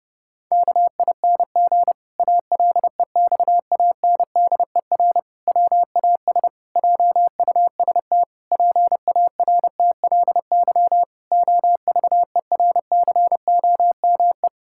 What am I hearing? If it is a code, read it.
Morse code, heard as a keyed tone, about 30 wpm, 700 Hz. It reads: KING ALEXANDER WAS JUST PARTLY OVERCOME
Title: Pangram Challenge v2 - 30wpm